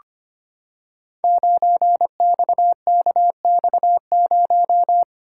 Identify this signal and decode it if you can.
Morse code, heard as a keyed tone, about 25 wpm, 700 Hz. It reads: 9XKX0
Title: Code Groups - 5 Letter:Numbers - 25wpm